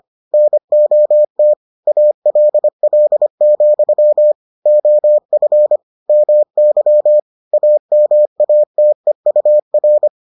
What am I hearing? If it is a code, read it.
Morse code, heard as a keyed tone, about 25 wpm, 600 Hz. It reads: NOT ALL, OF MY AMATEUR